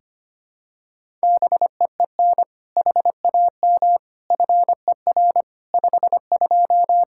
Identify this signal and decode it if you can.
Morse code, heard as a keyed tone, about 25 wpm, 700 Hz. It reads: BEEN HAM FER 52
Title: Practice QSO - Exchange 3 - More Info - 25wpm